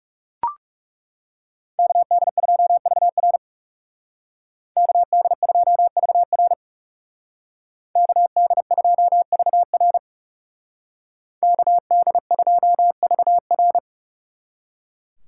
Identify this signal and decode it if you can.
Morse code, heard as a keyed tone, about 35 wpm, 700 Hz. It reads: KD2VR KD2VR KD2VR KD2VR
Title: Speed Racing - USA 2x2 Call Signs 30wpm